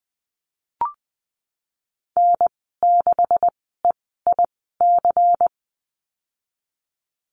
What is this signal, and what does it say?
Morse code, heard as a keyed tone, about 20 wpm, spaced as for 14 wpm, 700 Hz. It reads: N6EIC
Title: 128 Lesson 21g - Number 6 - Call Signs 20wpm